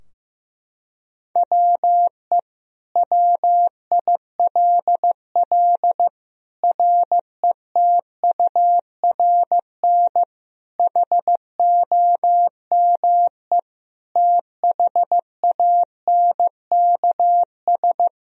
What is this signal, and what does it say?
Morse code, heard as a keyed tone, about 14 wpm, 700 Hz. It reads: WE WILL RETURN HOME THANKS
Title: Sentences from Top 1000 Words - Review Part 3 - 15wpm